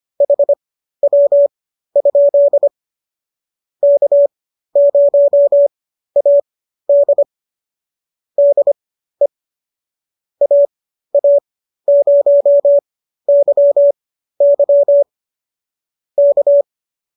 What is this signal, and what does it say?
Morse code, heard as a keyed tone, about 25 wpm, spaced as for 13 wpm, 570 Hz. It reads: HW? K0AD DE AA0YY K